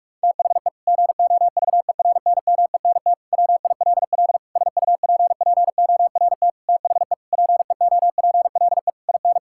Morse code, heard as a keyed tone, about 46 wpm, 700 Hz. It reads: THE GOVERNMENT WILL SUPPORT THE PEOPLE IN
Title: Sentences from Top 1000 Words - Review Part 5 - 45wpm